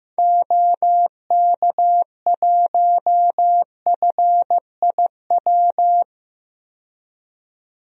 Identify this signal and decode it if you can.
Morse code, heard as a keyed tone, about 15 wpm, 700 Hz. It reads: OK1FIW
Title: DXCC-100 Call Signs - 6 Characters - 15wpm